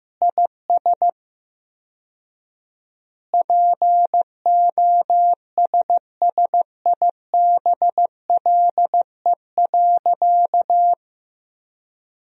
Morse code, heard as a keyed tone, about 15 wpm, 700 Hz. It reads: IS POSSIBLE.
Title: 4x Word-Spacing - Sentences from Top 500 Words - 15wpm